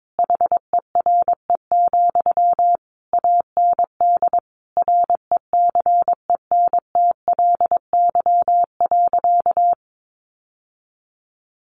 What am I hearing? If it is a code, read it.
Morse code, heard as a keyed tone, about 22 wpm, 700 Hz. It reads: HERE, AND RECENTLY.
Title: Sentences from Top 1200 Words 22wpm